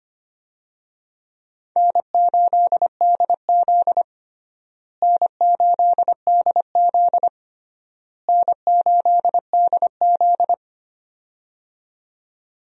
Morse code, heard as a keyed tone, about 25 wpm, 700 Hz. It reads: N8DZ N8DZ N8DZ
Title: USA Call Signs - 4 Characters - Repeated 3x - 25wpm